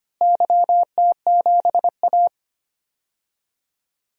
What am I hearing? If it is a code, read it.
Morse code, heard as a keyed tone, about 25 wpm, 700 Hz. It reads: YT7A
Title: DXCC-100 Call Signs - 4 Characters - Repeated 3x - 25wpm